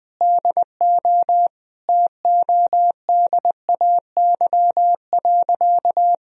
Morse code, heard as a keyed tone, about 21 wpm, 700 Hz. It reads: DO TODAY.
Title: Sentences from Top 1000 Words - Review Part 4 - 20wpm